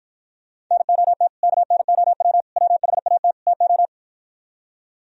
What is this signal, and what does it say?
Morse code, heard as a keyed tone, about 52 wpm, 700 Hz. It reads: NOT KNOW WHAT TO